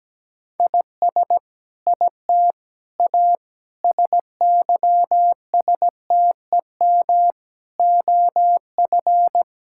Morse code, heard as a keyed tone, about 17 wpm, 700 Hz. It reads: IS IT A SYSTEM OF
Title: Sentences from Top 1000 Words - Review Part 6 - 17wpm